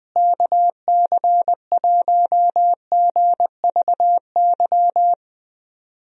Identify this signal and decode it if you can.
Morse code, heard as a keyed tone, about 21 wpm, 700 Hz. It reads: KC1GVY
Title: USA Call Signs - 6 Characters - Repeated 2x - 20wpm